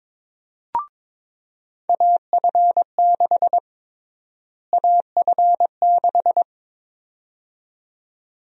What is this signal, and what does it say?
Morse code, heard as a keyed tone, about 22 wpm, 700 Hz. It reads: AF6 AF6
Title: USA Call Sign Prefixes - Repeated 2x - 22wpm